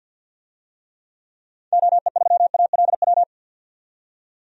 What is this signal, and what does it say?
Morse code, heard as a keyed tone, about 52 wpm, 700 Hz. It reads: OE3ALW
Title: DXCC-100 Call Signs - 6 Characters - 50wpm